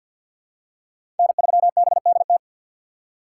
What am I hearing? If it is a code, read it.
Morse code, heard as a keyed tone, about 50 wpm, 700 Hz. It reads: N3BDT
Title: USA Call Signs - 5 Characters - 3x Character Spacing - 50wpm